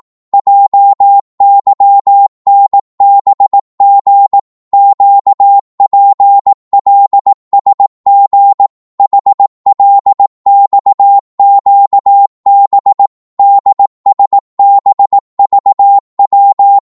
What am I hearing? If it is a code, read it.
Morse code, heard as a keyed tone, about 18 wpm, 800 Hz. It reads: JYNBG QPLSG HLXQB DSBVW
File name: Morse / ltr-18wpm